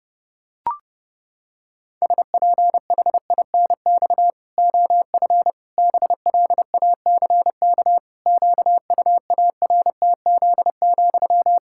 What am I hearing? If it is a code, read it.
Morse code, heard as a keyed tone, about 32 wpm, 700 Hz. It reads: SPHINX OF BLACK QUARTZ,
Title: Pangram Challenge v2 - 30wpm